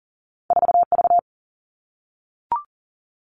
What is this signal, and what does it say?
Morse code, heard as a keyed tone, about 40 wpm, 700 Hz. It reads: 4V
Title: All Cut Numbers - 2-Digit 40wpm